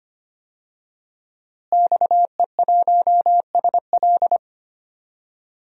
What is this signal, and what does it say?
Morse code, heard as a keyed tone, about 25 wpm, 700 Hz. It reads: XE1SL
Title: DXCC-100 Call Signs - 5 Characters - Repeated 3x - 25wpm